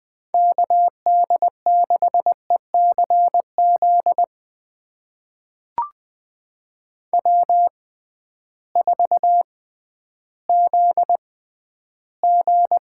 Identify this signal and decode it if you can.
Morse code, heard as a keyed tone, about 20 wpm, spaced as for 12 wpm, 700 Hz. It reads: KD6ECZ W4ZG
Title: USA Call Signs - 6x Character Spacing - 20wpm